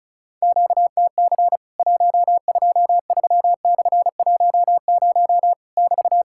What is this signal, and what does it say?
Morse code, heard as a keyed tone, about 36 wpm, 700 Hz. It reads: QTC 123/10 <BT>
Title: Worked All Europe DX Contest - QTC Exchange - 35wpm